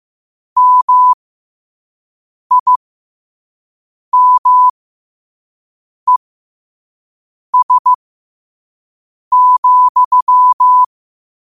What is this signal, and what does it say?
Morse code, heard as a keyed tone, about 15 wpm, spaced as for 5 wpm, 1000 Hz. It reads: MIMES,
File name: qotdfiles / 5wpm